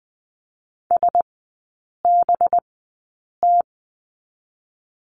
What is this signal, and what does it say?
Morse code, heard as a keyed tone, about 19 wpm, spaced as for 8 wpm, 700 Hz. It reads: SBT
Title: 184 Lesson 30e - Letter b - Three Characters 20wpm